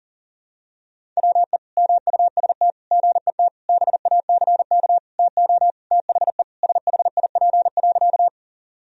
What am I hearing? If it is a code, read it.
Morse code, heard as a keyed tone, about 40 wpm, 700 Hz. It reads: WE MUST GET BACK TO THE SHIP.
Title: Sentences from Top 1000 Words - Review Part 2 - 40wpm